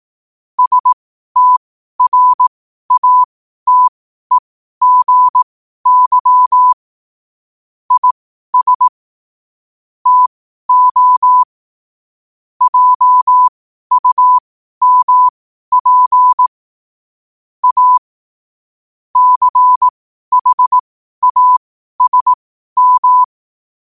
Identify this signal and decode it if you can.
Morse code, heard as a keyed tone, about 18 wpm, spaced as for 12 wpm, 1000 Hz. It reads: STRATEGY IS TO JUMP A CHASM
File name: qotdfiles / 12wpm